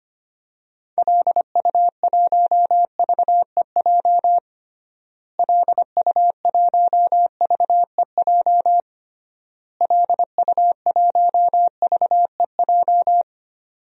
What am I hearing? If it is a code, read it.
Morse code, heard as a keyed tone, about 25 wpm, 700 Hz. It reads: LU1VEJ LU1VEJ LU1VEJ
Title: DXCC-100 Call Signs - 6 Characters - Repeated 3x - 25wpm